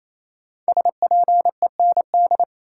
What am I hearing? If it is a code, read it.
Morse code, heard as a keyed tone, about 28 wpm, 700 Hz. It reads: SPEND
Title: Sets of 5 Words - New Word Drill - 28wpm